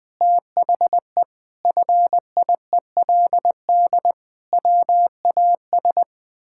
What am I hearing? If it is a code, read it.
Morse code, heard as a keyed tone, about 20 wpm, 700 Hz. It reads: THE FIELD WAS
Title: Sentences with 6-Word Phrases - 20wpm